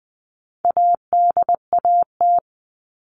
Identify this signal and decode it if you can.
Morse code, heard as a keyed tone, about 20 wpm, 700 Hz. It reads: ADAT
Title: All Cut Numbers - 4-Digit 20wpm